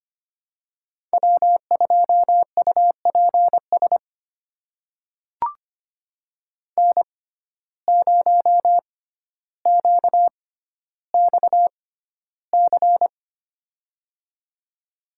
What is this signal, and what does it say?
Morse code, heard as a keyed tone, about 25 wpm, spaced as for 13 wpm, 700 Hz. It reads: W2UPS N0QXC
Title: USA Call Signs - 5 Characters - 6x Character Spacing - 25wpm